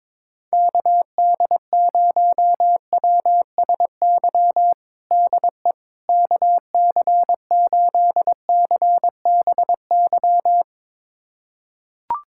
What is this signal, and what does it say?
Morse code, heard as a keyed tone, about 22 wpm, 700 Hz. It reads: KD0WSY DE KC8CBY
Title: USA Callsign DE Callsign 22wpm